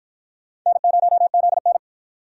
Morse code, heard as a keyed tone, about 53 wpm, 700 Hz. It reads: N0ZN